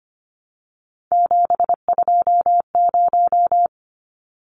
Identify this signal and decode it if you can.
Morse code, heard as a keyed tone, about 25 wpm, 700 Hz. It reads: 720
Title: Numbers - 3-Digit 25wpm